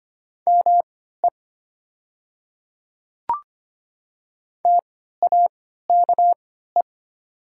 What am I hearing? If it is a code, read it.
Morse code, heard as a keyed tone, about 25 wpm, spaced as for 14 wpm, 700 Hz. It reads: ME TAKE
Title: 3x Character Spacing - Top 500 Words - 25wpm